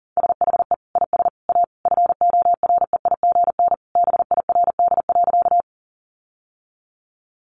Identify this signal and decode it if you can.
Morse code, heard as a keyed tone, about 40 wpm, 700 Hz. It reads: SHE IS A FOREIGN BIRD.
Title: Sentences from Top 1400 Words 40wpm